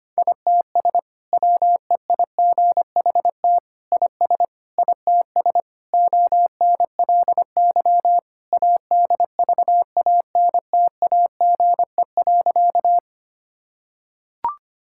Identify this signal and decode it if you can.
Morse code, heard as a keyed tone, about 25 wpm, 700 Hz. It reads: ITS WEIGHT IS ITS ONLY ADVANTAGE.
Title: Sentences from Top 1000 Words - Review Part 6 - 25wpm